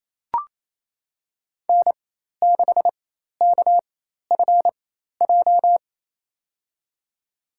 Morse code, heard as a keyed tone, about 28 wpm, spaced as for 13 wpm, 700 Hz. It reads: N6KFJ
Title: USA Call Signs - 5 Characters - 4x Character Spacing - 28wpm